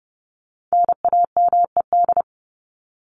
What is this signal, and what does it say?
Morse code, heard as a keyed tone, about 30 wpm, 700 Hz. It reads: NAMED
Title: ED Suffix - Top 100 Words with Base Words 30wpm